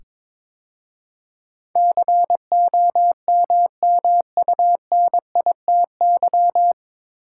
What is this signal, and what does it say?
Morse code, heard as a keyed tone, about 21 wpm, 700 Hz. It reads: COMMUNITY
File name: 9 Letter Words - First 3 Letters Given - 22wpm